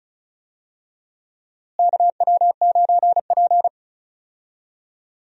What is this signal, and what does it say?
Morse code, heard as a keyed tone, about 36 wpm, 700 Hz. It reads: KW9P
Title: USA Call Signs - 4 Characters - Repeated 3x - 35wpm